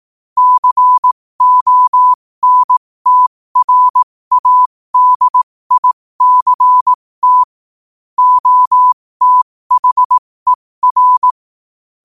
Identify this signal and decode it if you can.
Morse code, heard as a keyed tone, about 18 wpm, spaced as for 15 wpm, 1000 Hz. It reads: CONTRADICT OTHER
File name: qotdfiles / 15wpm